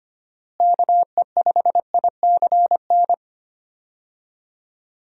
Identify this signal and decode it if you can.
Morse code, heard as a keyed tone, about 25 wpm, 700 Hz. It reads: KE5ICN
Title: USA Call Signs - 6 Characters - Repeated 2x - 25wpm